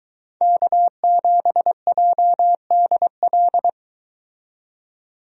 Morse code, heard as a keyed tone, about 22 wpm, 700 Hz. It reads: K7JDL